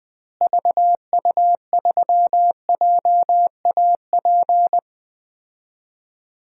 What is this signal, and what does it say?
Morse code, heard as a keyed tone, about 20 wpm, 700 Hz. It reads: VU3JAP
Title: DXCC-100 Call Signs - 6 Characters - Repeated 3x - 20wpm